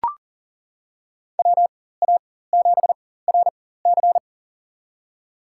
Morse code, heard as a keyed tone, about 40 wpm, spaced as for 19 wpm, 700 Hz. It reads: WA7RC
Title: USA Call Signs - 5 Characters - 4x Character Spacing - 40wpm